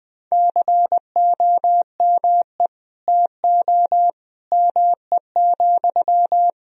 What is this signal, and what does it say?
Morse code, heard as a keyed tone, about 20 wpm, 700 Hz. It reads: COME TO ME,
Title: Sentences from Top 1000 Words - Review Part 2 - 20wpm